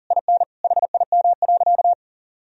Morse code, heard as a keyed tone, about 40 wpm, 700 Hz. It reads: IN HIM.
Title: Sentences from Top 1000 Words - Review Part 3 - 40wpm